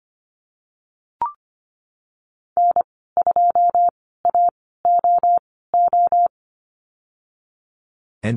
Morse code, heard as a keyed tone, about 25 wpm, spaced as for 16 wpm, 700 Hz. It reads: N2AOO
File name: Lesson 15g - Number 5 - Call Signs 25wpm